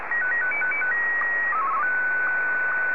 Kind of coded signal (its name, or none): SSTV (slow-scan television)